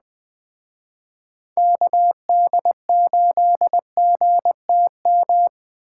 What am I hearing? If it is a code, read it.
Morse code, heard as a keyed tone, about 20 wpm, 700 Hz. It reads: KD8GTM
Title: Speed Racing - USA 2x3 Call Signs 17wpm